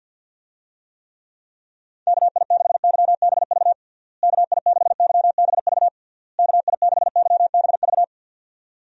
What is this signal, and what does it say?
Morse code, heard as a keyed tone, about 52 wpm, 700 Hz. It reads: KI6YBV KI6YBV KI6YBV
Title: USA Call Signs - 6 Characters - Repeated 3x - 50wpm